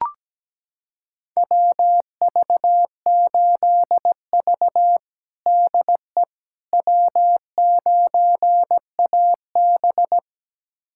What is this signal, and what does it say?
Morse code, heard as a keyed tone, about 17 wpm, 700 Hz. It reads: WV8V DE W9AB
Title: USA Callsign DE Callsign - 4 Characters - 17wpm